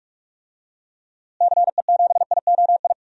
Morse code, heard as a keyed tone, about 47 wpm, 700 Hz. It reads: CE7IOI
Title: DXCC-100 Call Signs - 6 Characters - Repeated 3x - 45wpm